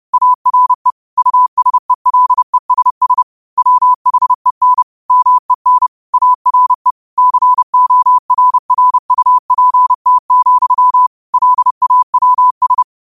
Morse code, heard as a keyed tone, about 30 wpm, 1000 Hz. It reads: ARE USELESS WHEN MEN ARE CORRUPT, LAWS
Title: 30 WPM morse code quotes for Thu, 29 Feb 2024